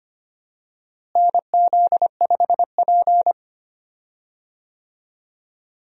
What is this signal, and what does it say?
Morse code, heard as a keyed tone, about 25 wpm, 700 Hz. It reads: NZ5P
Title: USA Call Signs - 4 Characters - Repeated 2x - 25wpm